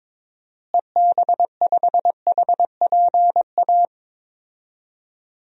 Morse code, heard as a keyed tone, about 21 wpm, 700 Hz. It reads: EB5HPA
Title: DXCC-100 Call Signs - 6 Characters - Repeated 3x - 22wpm